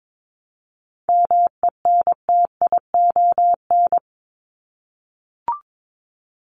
Morse code, heard as a keyed tone, about 22 wpm, 700 Hz. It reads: MENTION
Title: ED Suffix - Top 100 Words with Base Words 22wpm